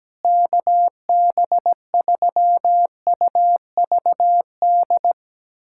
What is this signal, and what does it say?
Morse code, heard as a keyed tone, about 17 wpm, 700 Hz. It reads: KB3UVD
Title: USA Callsign DE Callsign 17wpm